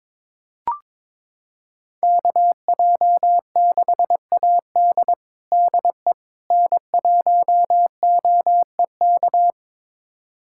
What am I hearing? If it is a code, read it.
Morse code, heard as a keyed tone, about 22 wpm, 700 Hz. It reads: KJ6AD DE N1OEK
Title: USA Callsign DE Callsign - 5 Characters - 22wpm